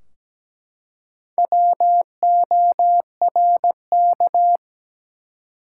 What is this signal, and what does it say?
Morse code, heard as a keyed tone, about 17 wpm, 700 Hz. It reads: WORK
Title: Four Letter Words - 17wpm